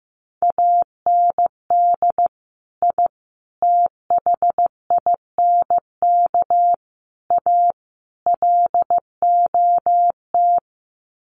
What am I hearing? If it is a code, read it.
Morse code, heard as a keyed tone, about 15 wpm, 700 Hz. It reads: AND I THINK A LOT
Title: Sets of 5 Words Encore2 15wpm